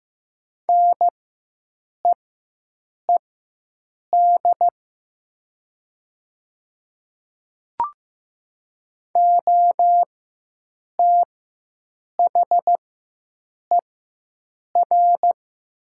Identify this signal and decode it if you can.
Morse code, heard as a keyed tone, about 15 wpm, spaced as for 7 wpm, 700 Hz. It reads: NEED OTHER
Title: 4x Character Spacing - Top 200 Words - 15wpm